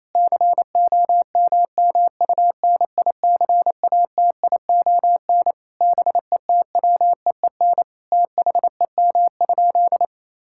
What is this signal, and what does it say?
Morse code, heard as a keyed tone, about 27 wpm, 700 Hz. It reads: COMMUNICATION BETWEEN THEM?
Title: Sentences from Top 1000 Words - Review Part 6 - 28wpm